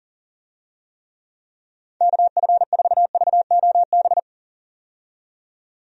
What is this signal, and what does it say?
Morse code, heard as a keyed tone, about 40 wpm, 700 Hz. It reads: KF4VOB
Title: USA Call Signs - 6 Characters - 40wpm